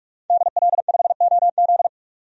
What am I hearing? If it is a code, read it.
Morse code, heard as a keyed tone, about 43 wpm, 700 Hz. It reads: DL5OZ
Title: DXCC-100 Call Signs - 5 Characters - 45wpm